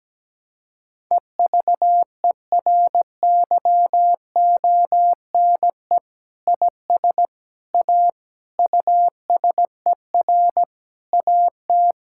Morse code, heard as a keyed tone, about 17 wpm, 700 Hz. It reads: EVERYONE IS A USER AT
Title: Sentences from Top 1000 Words - Review Part 5 - 17wpm